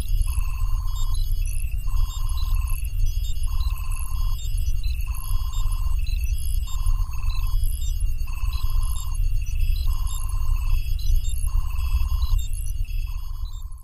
There is no code.